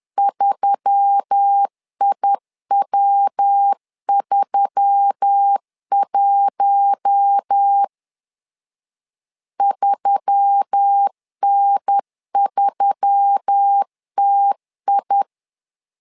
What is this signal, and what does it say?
Morse code, heard as a keyed tone, about 10 wpm, 800 Hz. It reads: 3IW31 3N3TI
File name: LEZIONI / 4 Q1S3 10wpm 2427